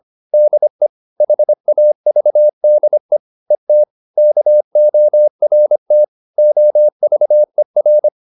Morse code, heard as a keyed tone, about 26 wpm, 600 Hz. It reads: DE HAVDE ET KORT OVER
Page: Ferie 25 wpm